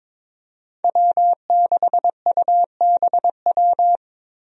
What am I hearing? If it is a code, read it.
Morse code, heard as a keyed tone, about 22 wpm, 700 Hz. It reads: W6UBW
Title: USA Call Signs - 5 Characters - Repeated 3x - 22wpm